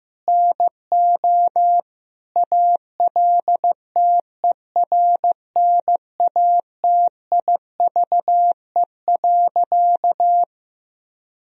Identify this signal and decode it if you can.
Morse code, heard as a keyed tone, about 15 wpm, 700 Hz. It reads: NO ALTERNATIVE.
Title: Sentences from Top 1000 Words - Review Part 4 - 15wpm